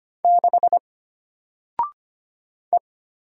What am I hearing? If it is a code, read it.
Morse code, heard as a keyed tone, about 25 wpm, 700 Hz. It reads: 6 E
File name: Single-Letter-Number-25wpm